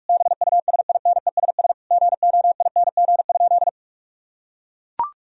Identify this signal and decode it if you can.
Morse code, heard as a keyed tone, about 41 wpm, 700 Hz. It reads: BUSINESS GOING?
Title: Sentences from Top 1000 Words - Review Part 4 - 45wpm